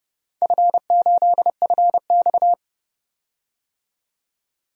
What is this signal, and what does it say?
Morse code, heard as a keyed tone, about 30 wpm, 700 Hz. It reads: F8FX
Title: DXCC-100 Call Signs - 4 Characters - 30wpm